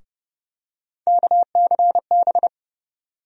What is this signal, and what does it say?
Morse code, heard as a keyed tone, about 30 wpm, 700 Hz. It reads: KCB
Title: USA Call Sign Postfixes - 3 Letters 30wpm